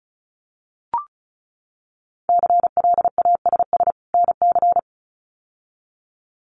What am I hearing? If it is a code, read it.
Morse code, heard as a keyed tone, about 34 wpm, 700 Hz. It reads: CLASS NC
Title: Winter Field Day Exchanges 35wpm